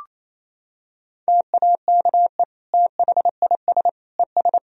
Morse code, heard as a keyed tone, about 28 wpm, 700 Hz. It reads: TAKE THIS ES
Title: Sentences from Top 200 Words and Abbreviations 28wpm